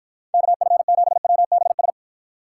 Morse code, heard as a keyed tone, about 54 wpm, 700 Hz. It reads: KF7WBS